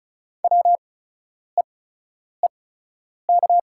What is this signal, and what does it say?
Morse code, heard as a keyed tone, about 36 wpm, spaced as for 10 wpm, 700 Hz. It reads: WEEK
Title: 8x Character Spacing - Top 500 Words - 35wpm